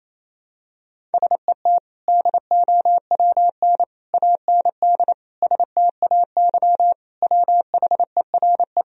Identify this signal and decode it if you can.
Morse code, heard as a keyed tone, about 28 wpm, 700 Hz. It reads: SET DOWN AND STAY WHERE
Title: Sentences from Top 1000 Words - Review Part 1 - 28wpm